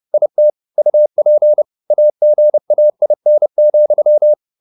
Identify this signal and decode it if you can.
Morse code, heard as a keyed tone, about 30 wpm, 600 Hz. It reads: IT UP AGAIN,